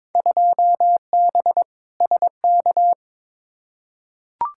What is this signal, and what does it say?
Morse code, heard as a keyed tone, about 22 wpm, 700 Hz. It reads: 2B SK
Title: Field Day Exchanges Encore - 22wpm